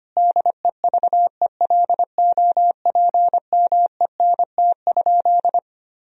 Morse code, heard as a keyed tone, about 25 wpm, 700 Hz. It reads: DEVELOPMENT?
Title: Sentences from Top 1000 Words - Review Part 4 - 25wpm